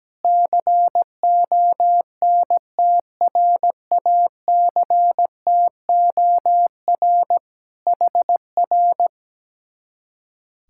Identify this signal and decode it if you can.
Morse code, heard as a keyed tone, about 17 wpm, 700 Hz. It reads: CONTRACTOR HR
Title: Speed-Racing Common QSO Phrases Encore - 17wpm